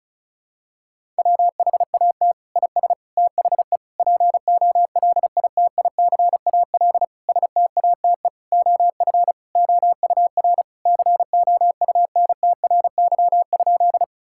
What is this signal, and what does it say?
Morse code, heard as a keyed tone, about 34 wpm, 700 Hz. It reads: WHAT IS THE POLITICAL STATE OF OUR COUNTRY?
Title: Sentences from Top 1000 Words - Review Part 5 - 35wpm